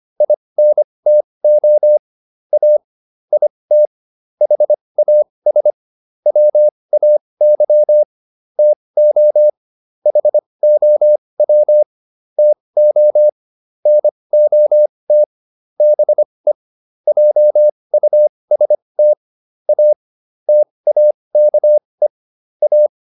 Morse code, heard as a keyed tone, about 25 wpm, spaced as for 20 wpm, 600 Hz. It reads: INTO A IT HAS WAY TO HOW TO NOT BE JUST A TAKE A
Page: advanced-practice-files